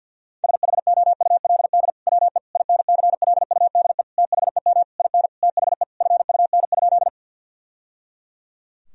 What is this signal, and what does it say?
Morse code, heard as a keyed tone, about 47 wpm, 700 Hz. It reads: SHOULD WE INCLUDE THEM IN THE FUN?
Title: Sentences from Top 1000 Words - Review Part 3 - 50wpm